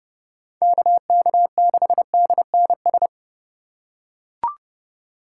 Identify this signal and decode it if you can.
Morse code, heard as a keyed tone, about 29 wpm, 700 Hz. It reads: KK6DNS